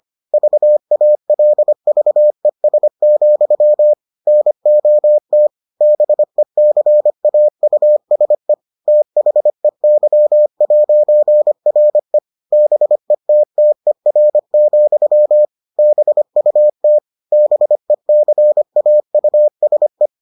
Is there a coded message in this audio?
Yes: Morse code from 0.3 to 20.1 s, heard as a keyed tone, about 25 wpm, 600 Hz. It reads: VALVES, NOT BECAUSE THEY'RE BETTER, BUT BECAUSE